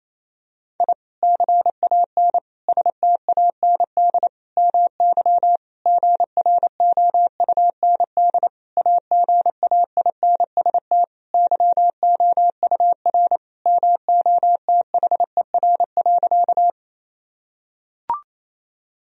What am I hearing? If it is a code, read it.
Morse code, heard as a keyed tone, about 27 wpm, 700 Hz. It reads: I CAN STAND MY GROUND AGAINST YOUR MOTHER.
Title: Sentences from Top 1000 Words - Review Part 2 - 28wpm